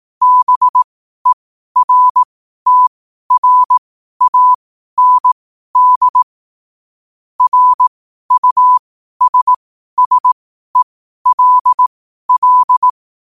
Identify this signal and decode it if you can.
Morse code, heard as a keyed tone, about 18 wpm, spaced as for 12 wpm, 1000 Hz. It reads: BERTRAND RUSSELL
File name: qotdfiles / 12wpm